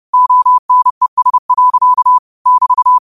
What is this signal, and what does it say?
Morse code, heard as a keyed tone, about 30 wpm, 1000 Hz. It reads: ONES. <BT>